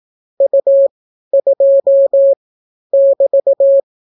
Morse code, heard as a keyed tone, about 18 wpm, 550 Hz. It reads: U 2 <BT>